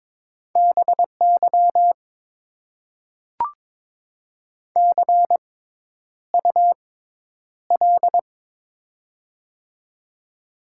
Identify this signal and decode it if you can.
Morse code, heard as a keyed tone, about 22 wpm, spaced as for 10 wpm, 700 Hz. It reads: BY CUL
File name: Top 90 QSO Elements - 6x Character Spacing - 22wpm